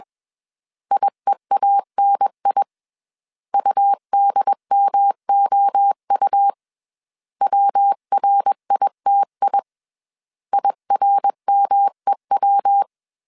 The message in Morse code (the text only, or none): IEANI UDMOU WRITI IRMEW